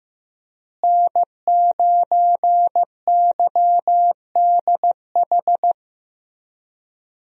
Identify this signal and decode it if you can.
Morse code, heard as a keyed tone, about 15 wpm, 700 Hz. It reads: N9YDH
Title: USA Call Signs - 5 Characters - Repeated 3x - 15wpm